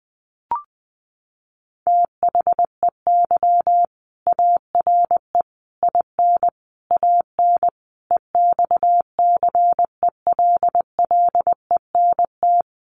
Morse code, heard as a keyed tone, about 20 wpm, 700 Hz. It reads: THEY ARE IN AN EXCELLENT
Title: Sentences from Top 1000 Words 20wpm